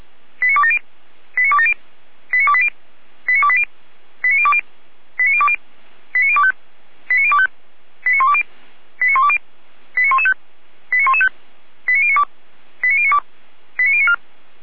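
Five-tone selective call (ZVEI), heard as quick runs of five tones, tones of about 70 ms. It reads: 89270 89270 89280 89280 89020 89020 89025 89025 89120 89120 89115 89115 89092 89092 89094
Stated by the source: ZVEI five-tone selective call 89208 received via SDR.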